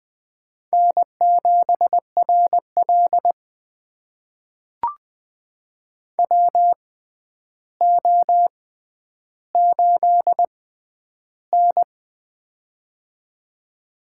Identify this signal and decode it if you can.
Morse code, heard as a keyed tone, about 20 wpm, spaced as for 10 wpm, 700 Hz. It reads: N7RL WO8N
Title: USA Call Signs - 4 Characters - 6x Character Spacing - 20wpm